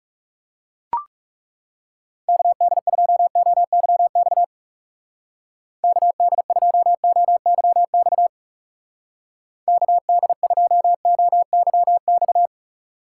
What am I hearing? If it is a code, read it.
Morse code, heard as a keyed tone, about 35 wpm, 700 Hz. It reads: KD2OYX KD2OYX KD2OYX
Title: Speed Racing - USA Call Signs 30wpm